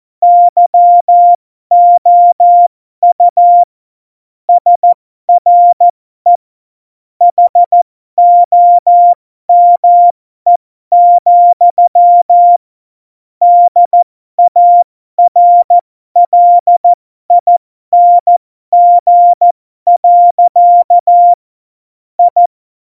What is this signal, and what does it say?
Morse code, heard as a keyed tone, about 14 wpm, spaced as for 12 wpm, 700 Hz. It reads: YOU SRE HOME, DARLING. I